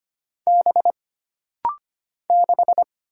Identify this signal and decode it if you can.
Morse code, heard as a keyed tone, about 25 wpm, 700 Hz. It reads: B 6
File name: Single Character (Letter:Number:Punctuation:Prosign) - Rapid-Fire - 25wpm